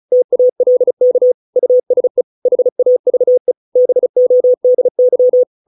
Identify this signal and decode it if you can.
Morse code, heard as a keyed tone, about 35 wpm, 500 Hz. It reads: TALK USE HAVE BODY